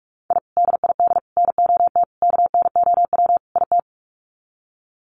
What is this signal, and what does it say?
Morse code, heard as a keyed tone, about 44 wpm, 700 Hz. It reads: I DID NOT KNOW IT
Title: Sets of 5 Words Encore2 45wpm